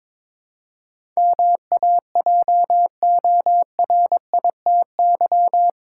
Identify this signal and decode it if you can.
Morse code, heard as a keyed tone, about 21 wpm, 700 Hz. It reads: MAJORITY
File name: practice / 22wpm / Eight Letter Words - 22wpm